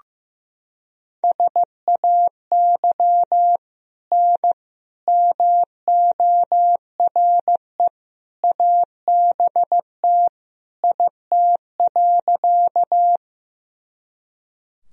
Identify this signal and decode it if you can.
Morse code, heard as a keyed tone, about 15 wpm, 700 Hz. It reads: SAY N MORE ABT IT.
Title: Sentences from Top 200 Words and Abbreviations 15wpm